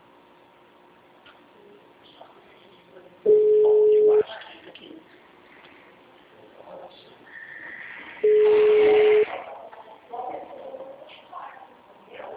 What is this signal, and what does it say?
Ringback tone (Europe, 425 Hz), with speech.
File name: out- -2003- - - .32453